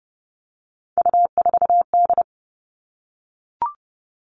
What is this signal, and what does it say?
Morse code, heard as a keyed tone, about 30 wpm, 700 Hz. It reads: U4D
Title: All Cut Numbers - 3-Digit 30wpm